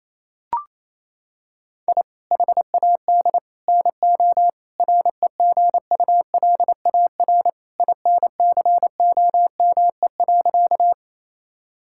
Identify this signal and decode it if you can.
Morse code, heard as a keyed tone, about 27 wpm, 700 Hz. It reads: I HAD NO REGULAR INCOME.
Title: Sentences from Top 1000 Words - Review Part 6 - 28wpm